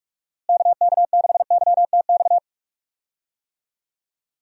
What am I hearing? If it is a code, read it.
Morse code, heard as a keyed tone, about 46 wpm, 700 Hz. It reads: KK6YTX
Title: USA Call Signs - 6 Characters - Repeated 2x - 45wpm